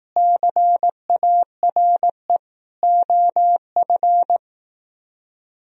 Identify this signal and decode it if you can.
Morse code, heard as a keyed tone, about 18 wpm, 700 Hz. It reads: CARE OF
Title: Speed-Racing - Sets of 4 Words 15wpm